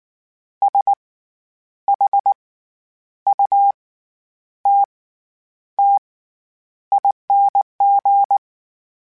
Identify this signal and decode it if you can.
Morse code, heard as a keyed tone, about 19 wpm, spaced as for 9 wpm, 800 Hz. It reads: SHUTTING
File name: Special INGs / Special INGs-mp3s / SlowToFaster ing @18